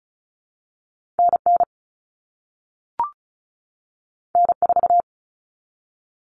Morse code, heard as a keyed tone, about 36 wpm, 700 Hz. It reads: NN N4
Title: All Cut Numbers - 2-Digit 35wpm